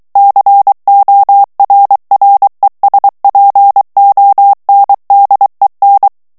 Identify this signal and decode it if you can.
Morse code, heard as a keyed tone, about 23 wpm, 800 Hz. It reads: CORRESPONDEN
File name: ESPAGNOL TEXTE EN CLAIR 2 VITESSE 22 MOTS MINUTES